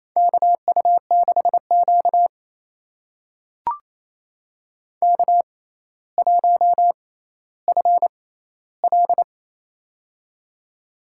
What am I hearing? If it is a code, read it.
Morse code, heard as a keyed tone, about 28 wpm, spaced as for 14 wpm, 700 Hz. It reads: KU6Q K1FL
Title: USA Call Signs - 4 Characters - 6x Character Spacing - 28wpm